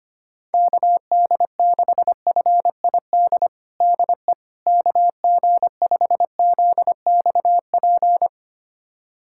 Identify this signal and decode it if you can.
Morse code, heard as a keyed tone, about 25 wpm, 700 Hz. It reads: KD6FID DE KG5ZXP
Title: USA Callsign DE Callsign - 6 Characters - 25wpm